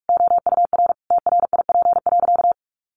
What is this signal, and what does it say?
Morse code, heard as a keyed tone, about 44 wpm, 700 Hz. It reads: OUR TRIP.
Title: Sentences from Top 1300 Words 45wpm